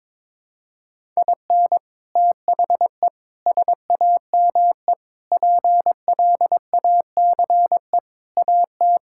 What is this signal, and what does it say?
Morse code, heard as a keyed tone, about 22 wpm, 700 Hz. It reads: IN THE SAME PLACE AT
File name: practice / 22wpm / Sets of 8 Words - 22wpm